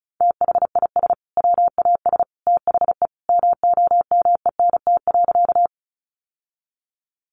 Morse code, heard as a keyed tone, about 34 wpm, 700 Hz. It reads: THIS WAS THE MOMENT.
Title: Sentences from Top 1000 Words 35wpm